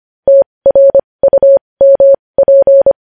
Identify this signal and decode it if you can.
Morse code, heard as a keyed tone, about 25 wpm, spaced as for 20 wpm, 550 Hz. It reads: TRUMP
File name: podcast / News headlines-1x-25x20